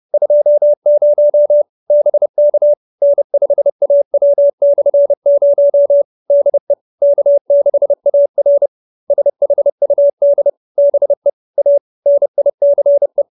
Morse code, heard as a keyed tone, about 30 wpm, 570 Hz. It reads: 20 BK N5AW/0 DE K6AR SHUD BE A NICE